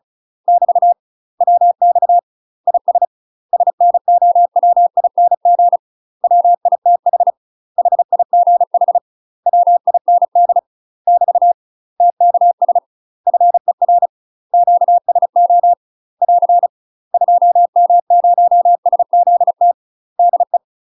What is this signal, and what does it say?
Morse code, heard as a keyed tone, about 36 wpm, 700 Hz. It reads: <BT> WX IS SNOWING WITH HIGH WIND <BT> TKS FER QSO <AR> 2M0SZT DE
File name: UK MASTER QSOs TRAINING / 35WPM / UK MASTER QSOs 0003